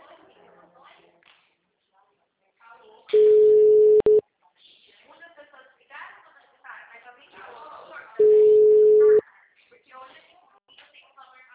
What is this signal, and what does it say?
Ringback tone (Europe, 425 Hz), with speech.
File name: out- -2031- - - .38516